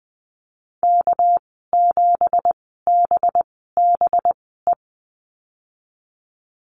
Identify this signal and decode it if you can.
Morse code, heard as a keyed tone, about 20 wpm, spaced as for 14 wpm, 700 Hz. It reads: K7BBE
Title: 213 Lesson 34g - Number 8 - Call Signs 20wpm